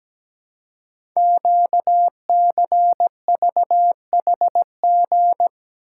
Morse code, heard as a keyed tone, about 17 wpm, 700 Hz. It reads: QCVHG
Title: Code Groups - 5 Letters - 17wpm